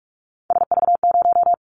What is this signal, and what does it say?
Morse code, heard as a keyed tone, about 44 wpm, 700 Hz. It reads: SV0
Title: Callsign Prefixes for Top 50 DXCC - 45wpm